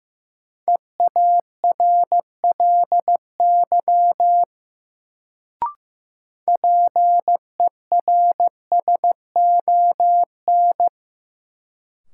Morse code, heard as a keyed tone, about 15 wpm, 700 Hz. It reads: EARLY PERSON
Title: Sentences from Top 400 Words - New Word Drill - 15wpm